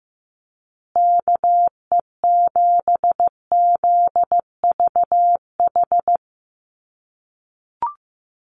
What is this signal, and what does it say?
Morse code, heard as a keyed tone, about 14 wpm, 700 Hz. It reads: KE7ZVH
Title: USA Call Signs Repeated 3x 15wpm